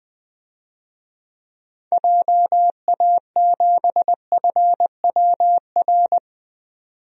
Morse code, heard as a keyed tone, about 21 wpm, 700 Hz. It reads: JA7FWR
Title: DXCC-100 Call Signs - 6 Characters - 20wpm